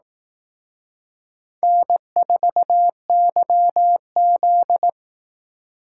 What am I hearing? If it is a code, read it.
Morse code, heard as a keyed tone, about 18 wpm, 700 Hz. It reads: N4YZ
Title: Speed Racing - USA 1x2 Call Signs 15wpm